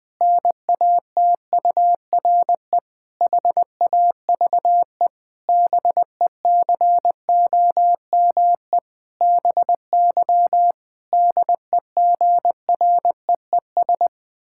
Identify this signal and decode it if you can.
Morse code, heard as a keyed tone, about 20 wpm, 700 Hz. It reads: NATURE HAVE BECOME BY DEGREES
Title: Pangram Challenge v2 - 20wpm